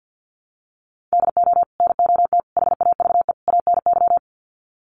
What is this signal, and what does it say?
Morse code, heard as a keyed tone, about 48 wpm, 700 Hz. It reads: DO NOT HAVE ANY
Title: Sets of 4 Words 50wpm